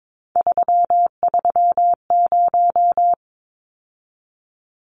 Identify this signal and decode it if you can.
Morse code, heard as a keyed tone, about 22 wpm, 700 Hz. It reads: 330
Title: Numbers - 3-Digit 22wpm